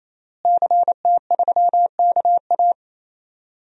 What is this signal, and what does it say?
Morse code, heard as a keyed tone, about 28 wpm, 700 Hz. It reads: CT3KA
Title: DXCC-100 Call Signs - 5 Characters - Repeated 3x - 28wpm